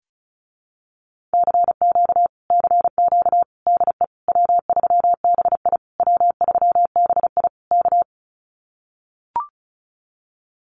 Morse code, heard as a keyed tone, about 34 wpm, 700 Hz. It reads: CQ CQ DE W3BI W3BI K
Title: Common QSO Phrases Encore 35wpm